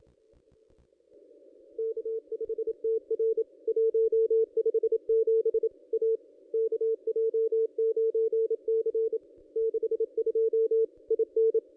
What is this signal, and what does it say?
Morse code, heard as a keyed tone, about 26 wpm, 440 Hz. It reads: K5TR 157 A KJ9C 62 IN
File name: contests / ss-cw-k5tr-op-n5rz -1500